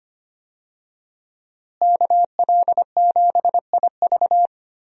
Morse code, heard as a keyed tone, about 25 wpm, 700 Hz. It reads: KL7IV
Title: USA Call Signs - 5 Characters - Repeated 3x - 25wpm